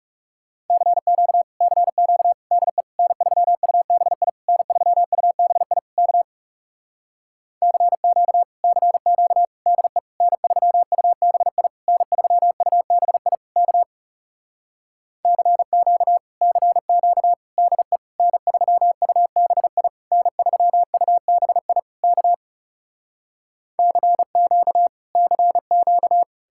Morse code, heard as a keyed tone, about 40 wpm, 700 Hz. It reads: CQ CQ DE N3UBI N3UBI K CQ CQ DE N3UBI N3UBI K CQ CQ DE N3UBI N3UBI K CQ CQ